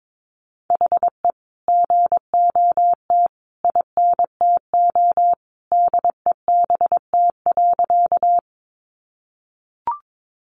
Morse code, heard as a keyed tone, about 22 wpm, 700 Hz. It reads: HE GOT INTO DEBT.
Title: Sentences from Top 1300 Words 22wpm